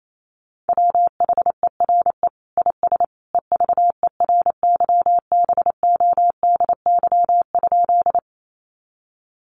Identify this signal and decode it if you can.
Morse code, heard as a keyed tone, about 28 wpm, 700 Hz. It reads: WHERE IS EVERYBODY?
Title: Sentences from Top 1400 Words 28wpm